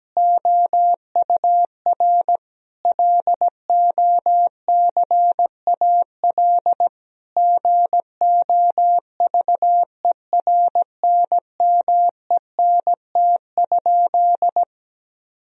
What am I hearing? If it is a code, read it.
Morse code, heard as a keyed tone, about 17 wpm, 700 Hz. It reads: OUR LOCAL GOVERNMENT?
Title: Sentences from Top 1000 Words - Review Part 4 - 17wpm